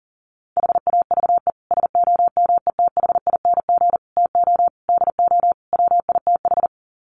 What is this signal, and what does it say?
Morse code, heard as a keyed tone, about 40 wpm, 700 Hz. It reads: HAVE SOMETHING TO DO WITH
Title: Sets of 5 Words Encore 40wpm